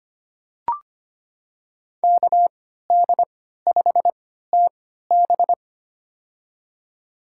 Morse code, heard as a keyed tone, about 25 wpm, spaced as for 14 wpm, 700 Hz. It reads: KD5TB
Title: USA Call Signs - 5 Characters - 3x Character Spacing - 25wpm